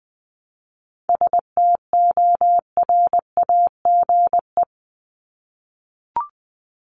Morse code, heard as a keyed tone, about 20 wpm, 700 Hz. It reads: STORAGE
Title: Top 1251-1500 Words 20wpm